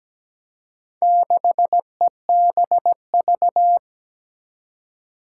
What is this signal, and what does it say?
Morse code, heard as a keyed tone, about 17 wpm, 700 Hz. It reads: 6EBV
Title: Four Challenging Letter-Number Combinations 17wpm